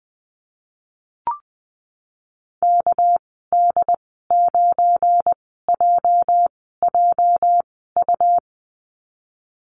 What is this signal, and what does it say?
Morse code, heard as a keyed tone, about 20 wpm, spaced as for 15 wpm, 700 Hz. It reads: KD9JJU
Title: 207 Lesson 33g - Letter j - Call Signs 20wpm